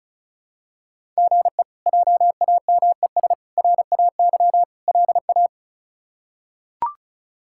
Morse code, heard as a keyed tone, about 36 wpm, 700 Hz. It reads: GE JAMES RAY LA
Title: K1USN SST QSO Practice - 35wpm